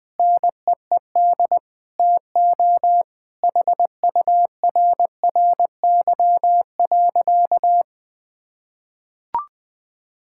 Morse code, heard as a keyed tone, about 20 wpm, 700 Hz. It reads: NEED TO HURRY.
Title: Sentences with 5-Word Phrases - 20wpm